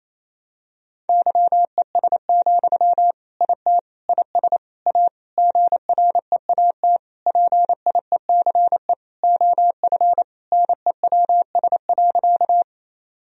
Morse code, heard as a keyed tone, about 28 wpm, 700 Hz. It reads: YES, IT IS A GREAT PIECE OF NEWS.
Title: Sentences from Top 1000 Words - Review Part 2 - 28wpm